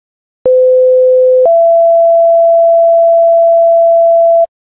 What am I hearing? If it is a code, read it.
Two-tone page, heard as a one-second tone, then a three-second tone, about 510 then 670 Hz. It is heard from 0.5 to 4.5 s.